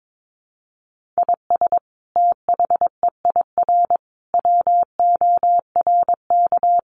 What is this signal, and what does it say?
Morse code, heard as a keyed tone, about 22 wpm, 700 Hz. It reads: IS THEIR WORK
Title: Sentences from Top 1500 Words 22wpm